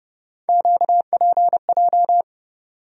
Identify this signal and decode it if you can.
Morse code, heard as a keyed tone, about 30 wpm, 700 Hz. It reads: QPJ
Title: USA Call Sign Postfixes - 3 Letters 30wpm